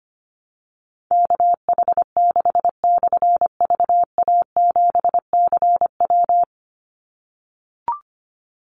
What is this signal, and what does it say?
Morse code, heard as a keyed tone, about 25 wpm, 700 Hz. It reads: KH6/VA7CW
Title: Call Signs - 25wpm